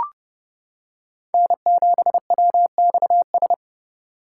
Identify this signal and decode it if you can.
Morse code, heard as a keyed tone, about 28 wpm, 700 Hz. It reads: N7WXS